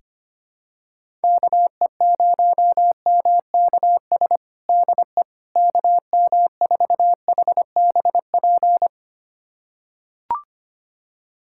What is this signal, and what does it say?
Morse code, heard as a keyed tone, about 25 wpm, 700 Hz. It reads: KE0MKS DE KM4HBP
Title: USA Callsign DE Callsign - 6 Characters - 25wpm